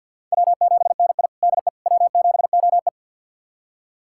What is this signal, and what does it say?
Morse code, heard as a keyed tone, about 46 wpm, 700 Hz. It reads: W7NI DE W7OE
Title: USA Callsign DE Callsign - 4 Characters - 50wpm